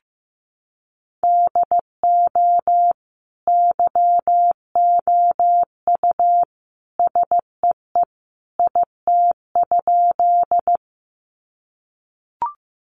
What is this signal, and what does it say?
Morse code, heard as a keyed tone, about 15 wpm, 700 Hz. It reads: DO YOU SEE IT?
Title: Sentences from Top 100 Words 15wpm